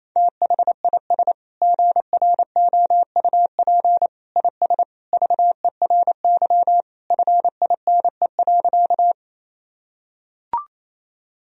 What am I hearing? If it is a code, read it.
Morse code, heard as a keyed tone, about 28 wpm, 700 Hz. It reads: THIS GROUP IS VERY FINE.
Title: Sentences from Top 1000 Words - Review Part 1 - 28wpm